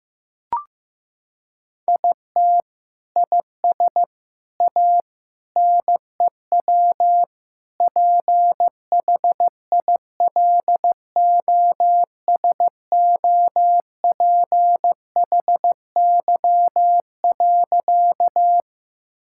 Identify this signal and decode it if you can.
Morse code, heard as a keyed tone, about 15 wpm, 700 Hz. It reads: IT IS A NEW PHILOSOPHY.
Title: Sentences from Top 1000 Words - Review Part 6 - 15wpm